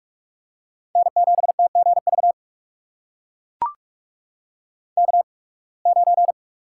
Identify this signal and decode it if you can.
Morse code, heard as a keyed tone, about 42 wpm, spaced as for 28 wpm, 700 Hz. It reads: N7TGV K9
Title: USA Call Signs - 5 Characters - 8x Character Spacing - 45wpm